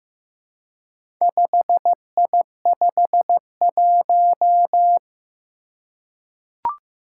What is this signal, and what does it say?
Morse code, heard as a keyed tone, about 15 wpm, 700 Hz. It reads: 5I51
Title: Four Challenging Letter-Number Combinations 15wpm